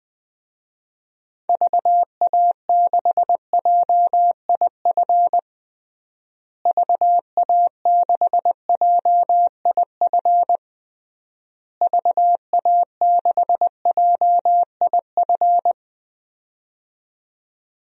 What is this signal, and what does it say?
Morse code, heard as a keyed tone, about 20 wpm, 700 Hz. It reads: VA6JIF VA6JIF VA6JIF
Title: DXCC-100 Call Signs - 6 Characters - Repeated 3x - 20wpm